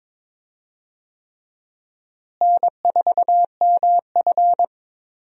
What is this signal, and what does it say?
Morse code, heard as a keyed tone, about 22 wpm, 700 Hz. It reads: N4MF
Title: USA Call Signs - 4 Characters - Repeated 3x - 22wpm